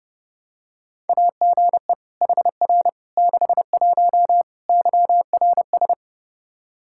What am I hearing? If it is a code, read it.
Morse code, heard as a keyed tone, about 30 wpm, 700 Hz. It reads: AGE HR 61 YRS
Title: Practice QSO - Exchange 3 - More Info - 30wpm